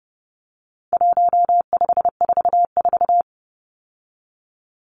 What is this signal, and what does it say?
Morse code, heard as a keyed tone, about 30 wpm, 700 Hz. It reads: 1544
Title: Numbers - 4-Digit 30wpm